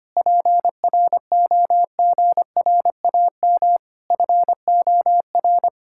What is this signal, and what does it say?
Morse code, heard as a keyed tone, about 25 wpm, 700 Hz. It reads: PROGRAM FOR
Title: Sentences from Top 1000 Words - Review Part 5 - 25wpm